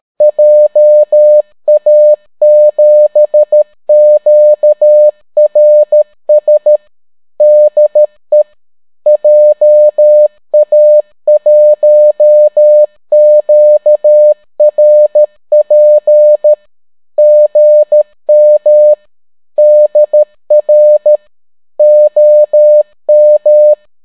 Morse code, heard as a keyed tone, about 13 wpm, 600 Hz. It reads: JA7QRS DE JA1QRP GM DR OM